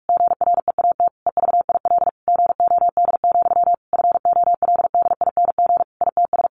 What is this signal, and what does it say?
Morse code, heard as a keyed tone, about 44 wpm, 700 Hz. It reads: GREAT EVIL GOD, FOLDING ITS
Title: A Sound Of Thunder 45wpm